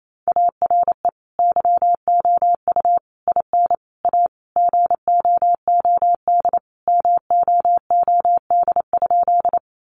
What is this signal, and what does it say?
Morse code, heard as a keyed tone, about 28 wpm, 700 Hz. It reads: ARE YOU IN A GOOD MOOD?
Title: Sentences from Top 1100 Words 28wpm